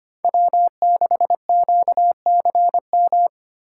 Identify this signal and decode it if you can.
Morse code, heard as a keyed tone, about 25 wpm, 700 Hz. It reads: W6QCM
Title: USA Call Signs - 5 Characters - 4x Character Spacing - 25wpm